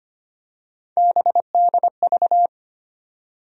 Morse code, heard as a keyed tone, about 25 wpm, 700 Hz. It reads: BDV
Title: USA Call Sign Postfixes - 3 Letters 25wpm